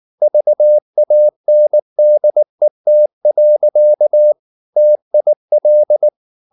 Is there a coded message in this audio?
Yes: Morse code from 0.2 to 6.1 s, heard as a keyed tone, about 19 wpm, 600 Hz. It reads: VANDET. TIL